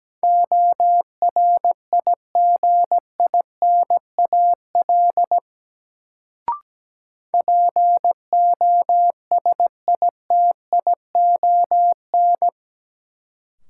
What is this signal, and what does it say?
Morse code, heard as a keyed tone, about 17 wpm, 700 Hz. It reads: ORIGINAL POSITION
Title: Sentences from Top 900 Words - New Word Drill - 17wpm